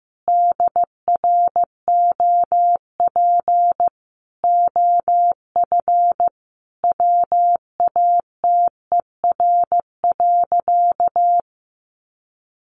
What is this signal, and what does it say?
Morse code, heard as a keyed tone, about 15 wpm, 700 Hz. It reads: DROP OF WATER.
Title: Sentences from Top 1400 Words 15wpm